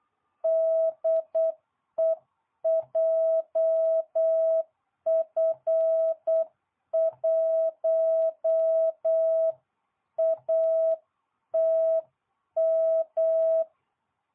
Morse code, heard as a keyed tone, about 8 wpm, 660 Hz. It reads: DE JF1ATM